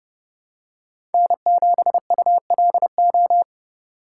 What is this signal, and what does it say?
Morse code, heard as a keyed tone, about 30 wpm, 700 Hz. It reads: N7ULO
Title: USA Call Signs - 5 Characters - 4x Character Spacing - 30wpm